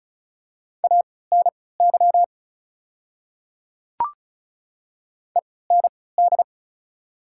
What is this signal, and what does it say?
Morse code, heard as a keyed tone, about 36 wpm, spaced as for 20 wpm, 690 Hz. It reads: ANY END
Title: 3x Character Spacing - Top 200 Words - 35wpm